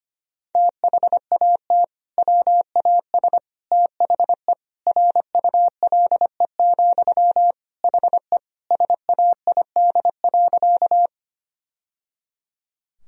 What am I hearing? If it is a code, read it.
Morse code, heard as a keyed tone, about 25 wpm, 700 Hz. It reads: THAT WAS THE RULE, HE SAID.
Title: Sentences from Top 1000 Words - Review Part 2 - 25wpm